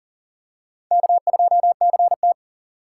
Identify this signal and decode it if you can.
Morse code, heard as a keyed tone, about 40 wpm, 700 Hz. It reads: K2CT
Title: USA Call Signs - 4 Characters - 4x Character Spacing - 40wpm